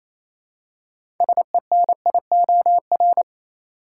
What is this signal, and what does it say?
Morse code, heard as a keyed tone, about 28 wpm, 700 Hz. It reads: SENIOR
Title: Six Letter Words - 28wpm